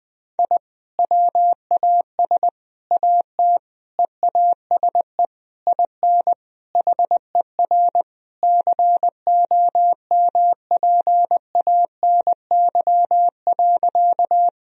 Morse code, heard as a keyed tone, about 20 wpm, 700 Hz. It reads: I WAS AT EASE IN HER COMPANY.
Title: Sentences from Top 1000 Words - Review Part 1 - 20wpm